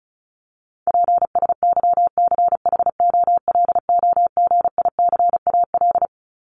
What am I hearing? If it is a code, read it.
Morse code, heard as a keyed tone, about 34 wpm, 700 Hz. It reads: PSYCHOLOGICAL
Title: Top 1251-1500 Words 35wpm